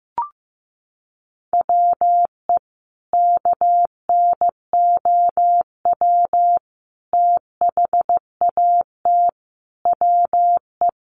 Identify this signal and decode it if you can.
Morse code, heard as a keyed tone, about 15 wpm, 700 Hz. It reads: WE KNOW THAT WE
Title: Sentences from Top 1100 Words 15wpm